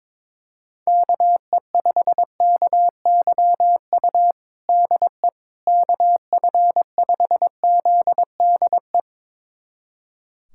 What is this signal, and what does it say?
Morse code, heard as a keyed tone, about 22 wpm, 700 Hz. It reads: KE5KYU DE KF5ZDE
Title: USA Callsign DE Callsign 22wpm